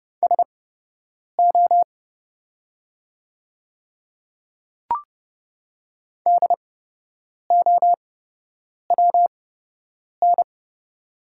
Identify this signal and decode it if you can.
Morse code, heard as a keyed tone, about 30 wpm, spaced as for 8 wpm, 700 Hz. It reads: SO DOWN
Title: 8x Character Spacing - Top 100 Words - 30wpm